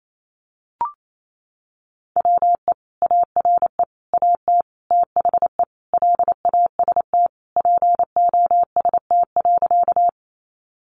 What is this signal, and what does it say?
Morse code, heard as a keyed tone, about 28 wpm, 700 Hz. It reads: WE ARE AT THE LAST POST.
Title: Sentences from Top 1000 Words 28wpm